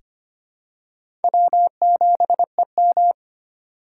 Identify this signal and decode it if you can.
Morse code, heard as a keyed tone, about 25 wpm, 700 Hz. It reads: W7EM
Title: USA Call Signs - 4 Characters - 3x Character Spacing - 25wpm